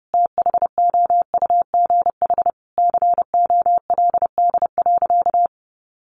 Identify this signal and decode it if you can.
Morse code, heard as a keyed tone, about 30 wpm, 700 Hz. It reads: THOUGH COLD.
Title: Sentences from Top 1000 Words 30wpm